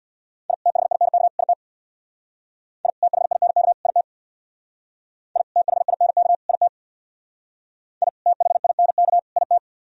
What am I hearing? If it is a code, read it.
Morse code, heard as a keyed tone, about 61 wpm, 700 Hz. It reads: I THINK IT I THINK IT I THINK IT I THINK IT
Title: Speed-Racing - Sets of 3 Words 50wpm